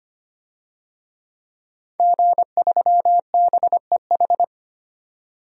Morse code, heard as a keyed tone, about 25 wpm, 700 Hz. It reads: G3BEH
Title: DXCC-100 Call Signs - 5 Characters - 25wpm